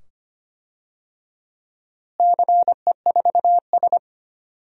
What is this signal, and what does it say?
Morse code, heard as a keyed tone, about 25 wpm, 700 Hz. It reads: CE4S
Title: DXCC-100 Call Signs - 4 Characters - Repeated 2x - 25wpm